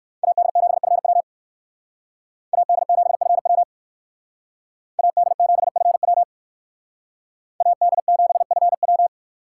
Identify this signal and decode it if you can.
Morse code, heard as a keyed tone, about 58 wpm, 700 Hz. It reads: AD7FW AD7FW AD7FW AD7FW